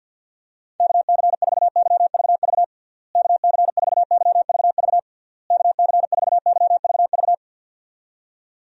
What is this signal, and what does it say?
Morse code, heard as a keyed tone, about 50 wpm, 700 Hz. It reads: KC4YVV KC4YVV KC4YVV
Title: USA Call Signs - 6 Characters - Repeated 3x - 50wpm